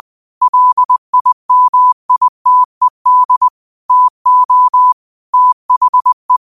Morse code, heard as a keyed tone, about 20 wpm, 1000 Hz. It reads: LIMITED TO THE
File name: qotdfiles / 20wpm